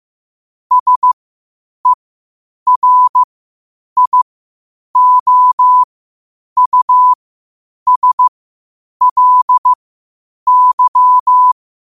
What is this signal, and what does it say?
Morse code, heard as a keyed tone, about 15 wpm, spaced as for 8 wpm, 1000 Hz. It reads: SERIOUSLY
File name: qotdfiles / 8wpm